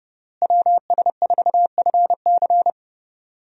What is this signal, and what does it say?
Morse code, heard as a keyed tone, about 32 wpm, 700 Hz. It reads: WS4FC
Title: USA Call Signs - 5 Characters - Repeated 3x - 30wpm